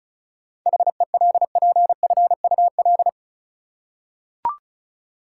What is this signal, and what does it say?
Morse code, heard as a keyed tone, about 36 wpm, 700 Hz. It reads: HELPFUL
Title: Seven Letter Words - 35wpm